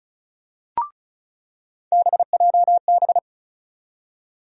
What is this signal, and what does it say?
Morse code, heard as a keyed tone, about 34 wpm, 700 Hz. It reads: BJB
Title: USA Call Sign Postfixes - 3 Letters 35wpm